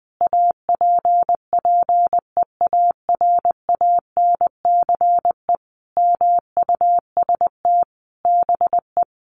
Morse code, heard as a keyed tone, about 20 wpm, 700 Hz. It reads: APPEARANCE MUST BE
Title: Sentences from Top 1100 Words 20wpm